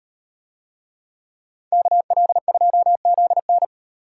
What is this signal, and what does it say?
Morse code, heard as a keyed tone, about 38 wpm, 700 Hz. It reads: KL2ZN